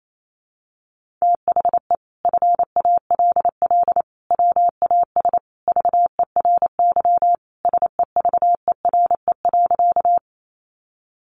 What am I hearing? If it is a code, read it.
Morse code, heard as a keyed tone, about 28 wpm, 700 Hz. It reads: THE FALL WAS VERY SEVERE.
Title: Sentences from Top 1400 Words 28wpm